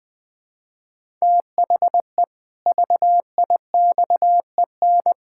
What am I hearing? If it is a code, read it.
Morse code, heard as a keyed tone, about 20 wpm, 700 Hz. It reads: THE VIXEN
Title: Pangram Challenge v2 - 20wpm